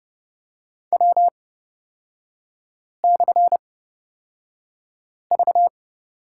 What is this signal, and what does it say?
Morse code, heard as a keyed tone, about 30 wpm, spaced as for 5 wpm, 700 Hz. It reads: W/V
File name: Lesson 36e - BT - Three Characters 30wpm